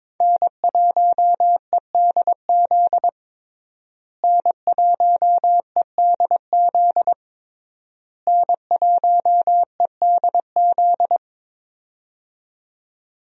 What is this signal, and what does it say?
Morse code, heard as a keyed tone, about 22 wpm, 700 Hz. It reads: N1EDZ N1EDZ N1EDZ
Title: USA Call Signs - 5 Characters - Repeated 3x - 22wpm